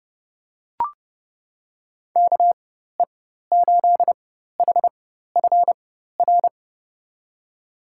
Morse code, heard as a keyed tone, about 30 wpm, spaced as for 14 wpm, 700 Hz. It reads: KE8HFR
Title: USA Call Signs - 4x Character Spacing - 30wpm